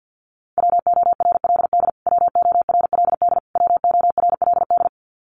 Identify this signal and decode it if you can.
Morse code, heard as a keyed tone, about 55 wpm, 700 Hz. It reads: WORLD WORLD WORLD
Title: Top 500 Words Repeated 50wpm